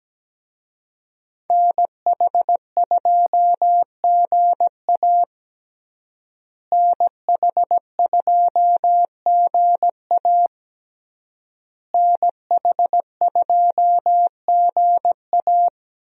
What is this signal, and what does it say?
Morse code, heard as a keyed tone, about 17 wpm, 700 Hz. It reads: NH2GA NH2GA NH2GA
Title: USA Call Signs - 5 Characters - Repeated 3x - 17wpm